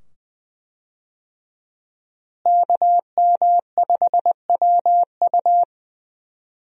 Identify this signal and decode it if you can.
Morse code, heard as a keyed tone, about 20 wpm, 700 Hz. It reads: KM5WU
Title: USA Call Signs - 5 Characters - Repeated 2x - 20wpm